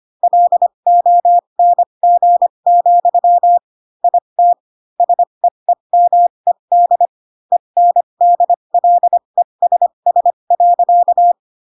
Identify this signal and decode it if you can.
Morse code, heard as a keyed tone, about 25 wpm, spaced as for 22 wpm, 700 Hz. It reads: LONG, IT SEEMED ENDLESS.
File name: Pinocchio / Pinocchio ch3 18 25 ik1hge